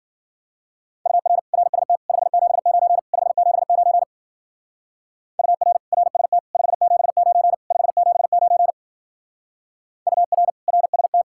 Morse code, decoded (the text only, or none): UR RST 579 579 UR RST 579 579 UR RST